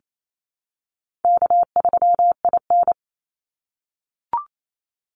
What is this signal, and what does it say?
Morse code, heard as a keyed tone, about 28 wpm, 700 Hz. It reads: K3IN
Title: Call Signs - 28wpm